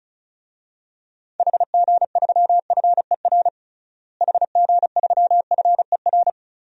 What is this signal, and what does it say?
Morse code, heard as a keyed tone, about 34 wpm, 700 Hz. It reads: HG3FER HG3FER
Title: DXCC-100 Call Signs - 6 Characters - Repeated 2x - 35wpm